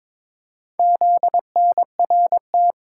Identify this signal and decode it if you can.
Morse code, heard as a keyed tone, about 22 wpm, 700 Hz. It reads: ZNRT
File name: practice / 22wpm / Four Letter-Number Combinations - 22wpm